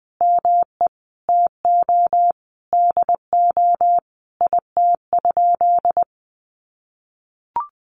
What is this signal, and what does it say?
Morse code, heard as a keyed tone, about 20 wpm, 700 Hz. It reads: ME TO DO IT?
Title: Sentences from Top 200 Words 20wpm